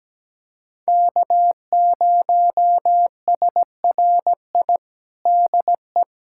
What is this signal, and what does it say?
Morse code, heard as a keyed tone, about 17 wpm, 700 Hz. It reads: K0SRI DE
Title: USA Callsign DE Callsign - 5 Characters - 17wpm